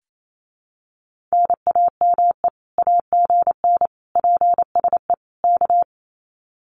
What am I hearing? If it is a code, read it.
Morse code, heard as a keyed tone, about 28 wpm, 700 Hz. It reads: NAME AGN PSE K
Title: Common QSO Phrases Encore 28wpm